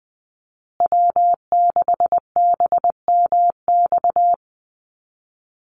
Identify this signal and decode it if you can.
Morse code, heard as a keyed tone, about 20 wpm, 700 Hz. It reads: W6BMX
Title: USA Call Signs Repeated 3x 20wpm